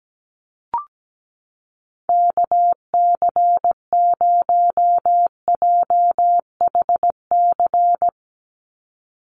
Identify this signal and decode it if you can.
Morse code, heard as a keyed tone, about 17 wpm, 700 Hz. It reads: KC0JHC
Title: USA Call Signs 17wpm